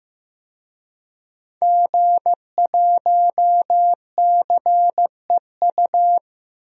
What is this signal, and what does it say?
Morse code, heard as a keyed tone, about 14 wpm, 700 Hz. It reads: G1CEU
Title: DXCC-100 Call Signs - 5 Characters - Repeated 2x - 15wpm